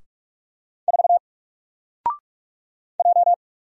Morse code, heard as a keyed tone, about 44 wpm, 700 Hz. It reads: 4 J
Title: Single Character (Letter:Number:Punctuation:Prosign) 45wpm